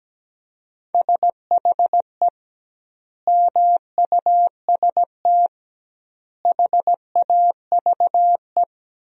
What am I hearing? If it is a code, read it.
Morse code, heard as a keyed tone, about 17 wpm, 700 Hz. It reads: SHE MUST HAVE
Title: 2x Word-Spacing - Sentences from Top 300 Words - 17wpm